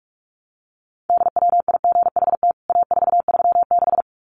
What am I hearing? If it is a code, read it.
Morse code, heard as a keyed tone, about 44 wpm, 700 Hz. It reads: DWIGHT A436
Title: CWT Exchange 45wpm